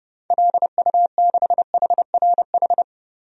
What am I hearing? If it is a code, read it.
Morse code, heard as a keyed tone, about 30 wpm, 700 Hz. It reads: LU6HRH
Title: DXCC-100 Call Signs - 6 Characters - 30wpm